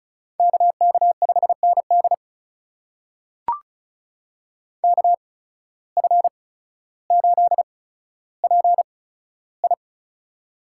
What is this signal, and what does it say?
Morse code, heard as a keyed tone, about 34 wpm, spaced as for 15 wpm, 700 Hz. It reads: KK5ND KF8PI
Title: USA Call Signs - 5 Characters - 8x Character Spacing - 35wpm